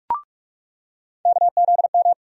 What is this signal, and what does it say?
Morse code, heard as a keyed tone, about 46 wpm, 700 Hz. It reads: KZM
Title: USA Call Sign Postfixes - 3 Letters 45wpm